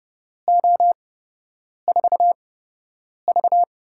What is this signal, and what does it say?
Morse code, heard as a keyed tone, about 30 wpm, spaced as for 8 wpm, 700 Hz. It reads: O4V
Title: USA Call Signs - 8x Character Spacing - 30wpm